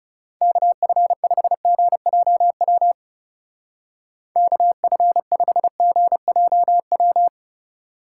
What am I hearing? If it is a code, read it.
Morse code, heard as a keyed tone, about 37 wpm, 700 Hz. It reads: KF5GJW KF5GJW
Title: Speed Racing - USA 2x3 Call Signs 30wpm